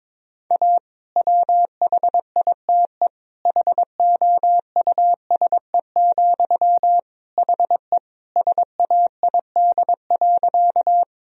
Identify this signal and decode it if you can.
Morse code, heard as a keyed tone, about 21 wpm, 700 Hz. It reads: A WHITE HOUSE, HE SAID.
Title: Sentences from Top 1000 Words - Review Part 1 - 22wpm